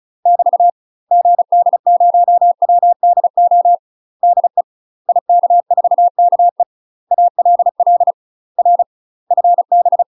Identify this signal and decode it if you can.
Morse code, heard as a keyed tone, about 34 wpm, 700 Hz. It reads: <BT> GD0WDO DE IK4KE ALL R FB
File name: UK MASTER QSOs TRAINING / 35WPM / UK MASTER QSOs 0034